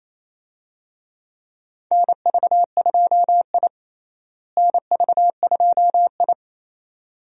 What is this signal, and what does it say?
Morse code, heard as a keyed tone, about 28 wpm, 700 Hz. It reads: NV2I NV2I
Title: USA Call Signs - 4 Characters - Repeated 2x - 28wpm